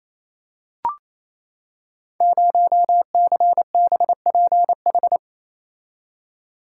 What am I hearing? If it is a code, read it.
Morse code, heard as a keyed tone, about 29 wpm, 700 Hz. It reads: 0CBPH
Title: Code Groups - 5 Letter:Numbers - 28wpm